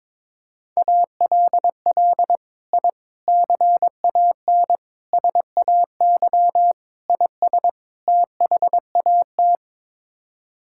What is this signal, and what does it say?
Morse code, heard as a keyed tone, about 22 wpm, 700 Hz. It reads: ALL I CAN SAY IS THAT
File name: practice / 22wpm / Sets of 6 Words - 22wpm